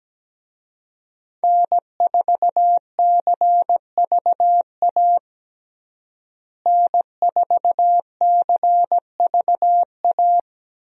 Morse code, heard as a keyed tone, about 17 wpm, 700 Hz. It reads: N4CVA N4CVA
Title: USA Call Signs - 5 Characters - Repeated 3x - 17wpm